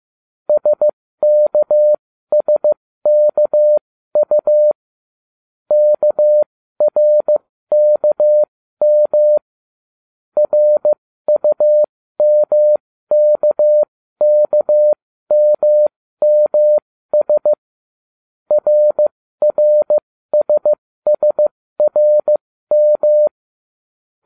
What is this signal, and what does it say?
Morse code, heard as a keyed tone, about 15 wpm, spaced as for 13 wpm, 600 Hz. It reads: SKSKU KRKM RUMKKMMS RRSSRM